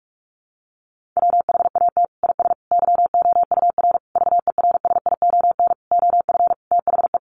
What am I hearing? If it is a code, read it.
Morse code, heard as a keyed tone, about 44 wpm, 700 Hz. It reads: WHAT IS YOUR VERSION OF THE
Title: Sentences from Top 1400 Words 45wpm